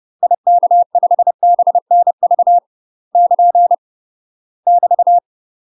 Morse code, heard as a keyed tone, about 30 wpm, 700 Hz. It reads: IK5BNV <KN> <BT>
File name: UK MASTER QSOs TRAINING / 30WPM / UK MASTER QSOs 0004